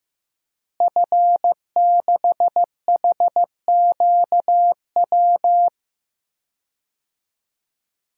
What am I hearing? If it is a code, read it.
Morse code, heard as a keyed tone, about 15 wpm, 700 Hz. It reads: F6HQW
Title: DXCC-100 Call Signs - 5 Characters - Repeated 2x - 15wpm